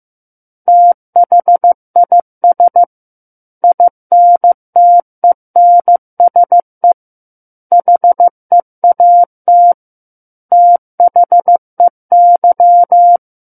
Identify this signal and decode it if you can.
Morse code, heard as a keyed tone, about 15 wpm, 700 Hz. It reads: THIS INTENSE HEAT THEY
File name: Book One / 06-Chapter 6 - 15 WPM